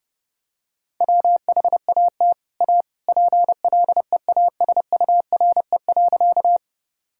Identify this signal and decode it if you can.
Morse code, heard as a keyed tone, about 32 wpm, 700 Hz. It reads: WHAT A PLEASURE.
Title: Sentences from Top 1600 Words 30wpm